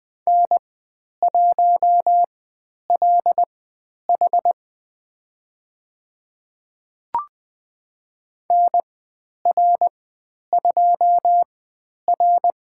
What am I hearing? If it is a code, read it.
Morse code, heard as a keyed tone, about 20 wpm, spaced as for 10 wpm, 700 Hz. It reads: N1LH NR2R
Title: 102 Lesson 17g - Letter u - Call Signs 20wpm